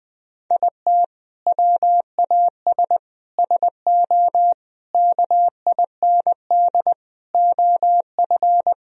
Morse code, heard as a keyed tone, about 19 wpm, 700 Hz. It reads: IT WAS SO KIND OF
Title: Sentences from Top 1000 Words - Review Part 3 - 20wpm